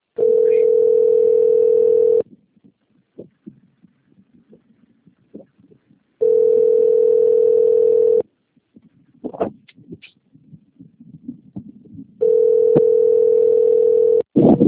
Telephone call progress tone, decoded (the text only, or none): ringback tone